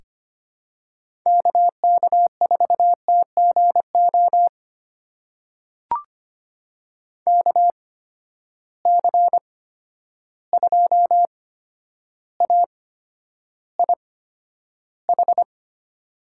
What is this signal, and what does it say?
Morse code, heard as a keyed tone, about 25 wpm, spaced as for 11 wpm, 700 Hz. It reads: KK4TGO KC2AIH
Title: USA Call Signs - 6 Characters - 8x Character Spacing - 25wpm